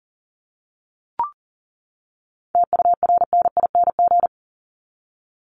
Morse code, heard as a keyed tone, about 40 wpm, 700 Hz. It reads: TURNING
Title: ING Suffix - Top 100 Words with Base Words 40wpm